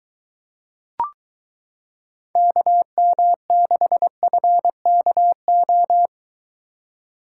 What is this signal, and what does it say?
Morse code, heard as a keyed tone, about 22 wpm, 700 Hz. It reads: KM6FKO